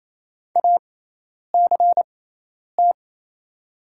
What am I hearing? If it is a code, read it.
Morse code, heard as a keyed tone, about 27 wpm, spaced as for 10 wpm, 700 Hz. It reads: ACT
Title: 6x Character Spacing - Top 200 Words - 28wpm